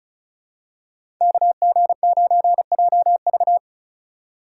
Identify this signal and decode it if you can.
Morse code, heard as a keyed tone, about 36 wpm, 700 Hz. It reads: KG9JV
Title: USA Call Signs - 5 Characters - 2x Character Spacing - 35wpm